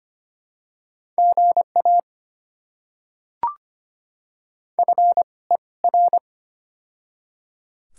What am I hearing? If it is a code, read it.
Morse code, heard as a keyed tone, about 25 wpm, spaced as for 20 wpm, 700 Hz. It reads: GA FER
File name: Top 20 QSO Elements - 2x Character Spacing - 25wpm